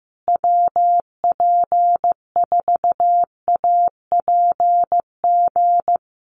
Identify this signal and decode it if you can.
Morse code, heard as a keyed tone, about 15 wpm, 700 Hz. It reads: WP4APG
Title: USA Call Signs Repeated 3x 15wpm